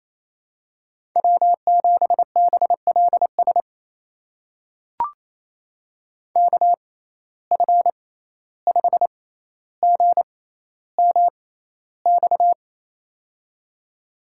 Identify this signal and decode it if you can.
Morse code, heard as a keyed tone, about 28 wpm, spaced as for 14 wpm, 700 Hz. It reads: W7BLS KF5GMX
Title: USA Call Signs - 6x Character Spacing - 28wpm